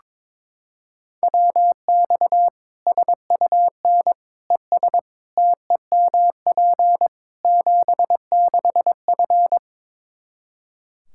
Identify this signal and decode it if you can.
Morse code, heard as a keyed tone, about 22 wpm, 700 Hz. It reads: WX SUN ES TEMP 76F
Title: K1USN SST QSO Practice 22wpm